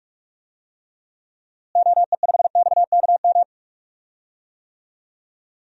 Morse code, heard as a keyed tone, about 44 wpm, 700 Hz. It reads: OE5XKM
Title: DXCC-100 Call Signs - 6 Characters - Repeated 3x - 45wpm